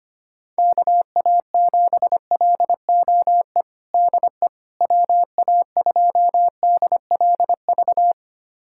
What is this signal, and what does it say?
Morse code, heard as a keyed tone, about 25 wpm, 700 Hz. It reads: KA7LOE DE WA2DLV
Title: USA Callsign DE Callsign - 6 Characters - 25wpm